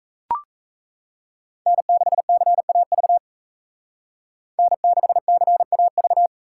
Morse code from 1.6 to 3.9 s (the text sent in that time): N6CAV